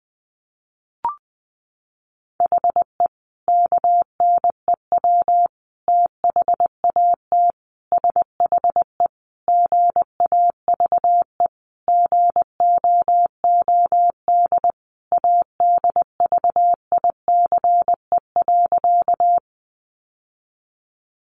Morse code, heard as a keyed tone, about 20 wpm, 700 Hz. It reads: HE KNEW THAT SHE GAVE GOOD ADVICE.
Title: Sentences from Top 1100 Words 20wpm